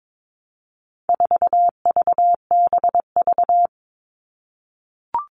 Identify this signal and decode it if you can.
Morse code, heard as a keyed tone, about 22 wpm, 700 Hz. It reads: 4VBV
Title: All Cut Numbers - 4-Digit 22wpm